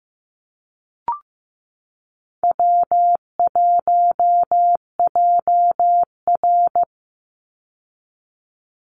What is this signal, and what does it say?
Morse code, heard as a keyed tone, about 15 wpm, 700 Hz. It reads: W1JR
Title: Easy Call Signs Repeated 3x - 15wpm